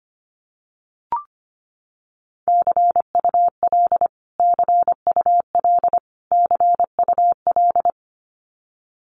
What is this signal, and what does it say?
Morse code, heard as a keyed tone, about 25 wpm, 700 Hz. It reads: CUL CUL CUL
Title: Abbreviations Repeated 3x - 25wpm